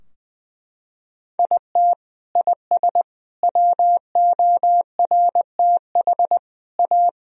Morse code, heard as a keyed tone, about 21 wpm, 700 Hz. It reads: IT IS WORTH A
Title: Sentences from Top 1000 Words - Review Part 3 - 20wpm